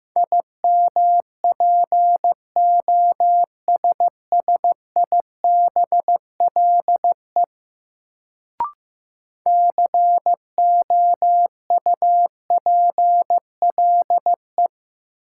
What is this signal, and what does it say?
Morse code, heard as a keyed tone, about 15 wpm, 700 Hz. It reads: IMPOSSIBLE COUPLE
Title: Sentences from Top 1100 Words - New Word Drill Alt - 15wpm